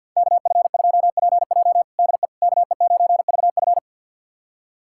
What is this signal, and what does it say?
Morse code, heard as a keyed tone, about 53 wpm, 700 Hz. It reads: KF2PJ DE KE9VF
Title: USA Callsign DE Callsign - 5 Characters - 50wpm